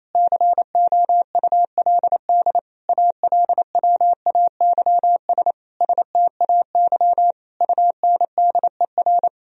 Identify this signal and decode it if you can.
Morse code, heard as a keyed tone, about 28 wpm, 700 Hz. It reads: COULD ALWAYS STAY UNDER
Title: Sentences from Top 1000 Words - Review Part 1 - 28wpm